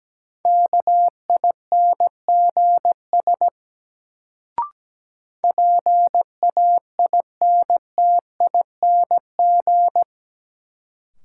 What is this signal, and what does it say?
Morse code, heard as a keyed tone, about 17 wpm, 700 Hz. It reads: KINGS PAINTING
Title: Sentences from Top 1000 Words - Review Part 6 - New Word Drill - 17wpm